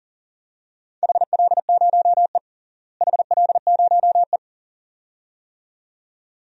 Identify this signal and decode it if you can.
Morse code, heard as a keyed tone, about 37 wpm, 700 Hz. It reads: HL0E HL0E
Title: DXCC-100 Call Signs - 4 Characters - Repeated 2x - 40wpm